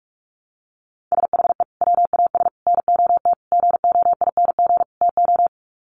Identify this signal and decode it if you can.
Morse code, heard as a keyed tone, about 44 wpm, 700 Hz. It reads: SHE WAS NOT GOING TO
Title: Sets of 5 Words Encore2 45wpm